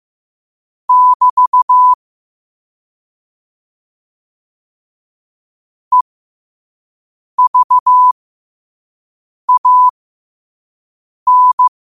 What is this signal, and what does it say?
Morse code, heard as a keyed tone, about 15 wpm, spaced as for 5 wpm, 1000 Hz. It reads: <BT> EVAN
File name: qotdfiles / 5wpm